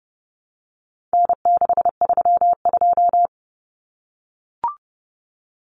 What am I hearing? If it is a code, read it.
Morse code, heard as a keyed tone, about 30 wpm, 700 Hz. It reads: N632
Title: Common Cut Numbers - 4-Digit 30wpm